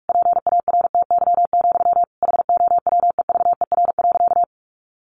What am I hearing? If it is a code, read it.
Morse code, heard as a keyed tone, about 44 wpm, 700 Hz. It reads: PARTY, HOWEVER.
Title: Sentences from Top 1400 Words 45wpm